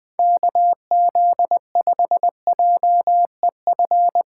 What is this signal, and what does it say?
Morse code, heard as a keyed tone, about 20 wpm, 700 Hz. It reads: KZ5JEF
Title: USA Call Signs - 6 Characters - 4x Character Spacing - 20wpm